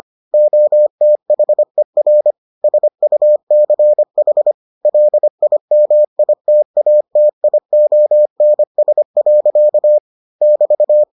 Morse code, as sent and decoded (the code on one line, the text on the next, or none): --- - .... .-. / ... ..- -.-. .... / .-.. .. -- .. - .- - .. --- -. ... .-.-.- / -...-
OTHER SUCH LIMITATIONS. <BT>